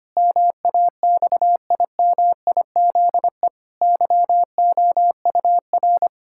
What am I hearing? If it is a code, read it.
Morse code, heard as a keyed tone, about 25 wpm, 700 Hz. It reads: MAXIMIZE YOUR
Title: Pangram Challenge v2 - 25wpm